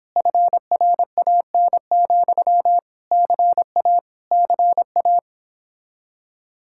Morse code, heard as a keyed tone, about 29 wpm, 700 Hz. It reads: FRAN, CA CA